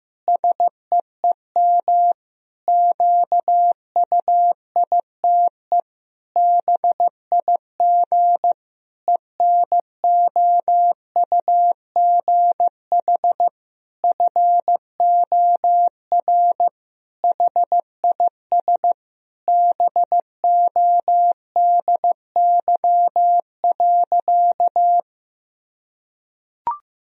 Morse code, heard as a keyed tone, about 15 wpm, 700 Hz. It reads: SEEM QUITE BIG ENOUGH FOR HIS BODY.
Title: Sentences from Top 1000 Words - Review Part 2 - 15wpm